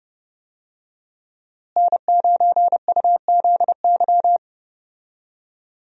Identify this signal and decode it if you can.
Morse code, heard as a keyed tone, about 29 wpm, 700 Hz. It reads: N9UZY
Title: USA Call Signs - 5 Characters - Repeated 3x - 30wpm